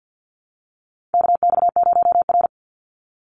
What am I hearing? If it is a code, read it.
Morse code, heard as a keyed tone, about 50 wpm, 700 Hz. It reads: KX0R
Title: NA SOTA Chaser Callsigns 50wpm